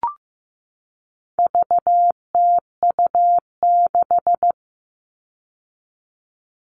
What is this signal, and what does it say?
Morse code, heard as a keyed tone, about 15 wpm, 700 Hz. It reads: VTU6
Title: All Cut Numbers - 4-Digit 15wpm